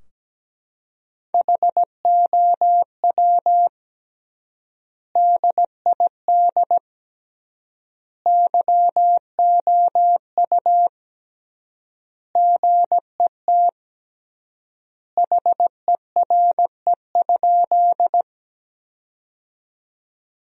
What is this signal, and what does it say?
Morse code, heard as a keyed tone, about 17 wpm, 700 Hz. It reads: HOW DID YOU GET HERE?
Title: 3x Word-Spacing - Sentences from Top 200 Words - 17wpm